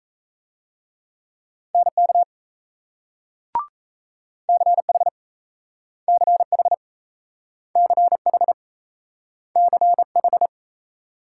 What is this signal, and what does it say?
Morse code, heard as a keyed tone, about 38 wpm, 700 Hz. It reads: NK CH CH CH CH